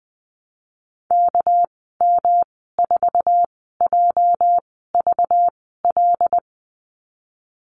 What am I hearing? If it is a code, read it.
Morse code, heard as a keyed tone, about 20 wpm, spaced as for 14 wpm, 700 Hz. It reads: KM4JVL
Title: 245 Lesson 39g - Letter z - Call Signs 20wpm